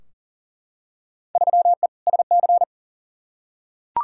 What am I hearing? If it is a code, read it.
Morse code, heard as a keyed tone, about 40 wpm, 700 Hz. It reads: 3E SC
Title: Field Day Exchanges Encore - 40wpm